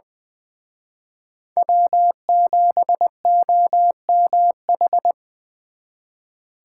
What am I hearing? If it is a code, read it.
Morse code, heard as a keyed tone, about 24 wpm, 700 Hz. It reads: W7OMH
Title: Speed Racing - USA 1x3 Call Signs 20wpm